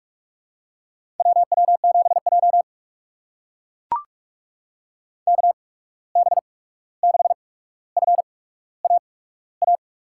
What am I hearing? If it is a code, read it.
Morse code, heard as a keyed tone, about 44 wpm, spaced as for 17 wpm, 700 Hz. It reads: WW7J KB6FAA
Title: USA Call Signs - 8x Character Spacing - 45wpm